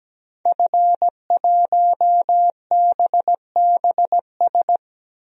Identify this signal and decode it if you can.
Morse code, heard as a keyed tone, about 17 wpm, 700 Hz. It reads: F1BBS
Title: DXCC-100 Call Signs - 5 Characters - Repeated 3x - 17wpm